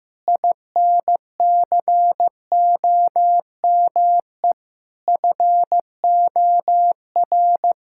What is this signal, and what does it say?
Morse code, heard as a keyed tone, about 15 wpm, 700 Hz. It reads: INCOME FOR
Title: Sentences from Top 1000 Words - Review Part 5 - 15wpm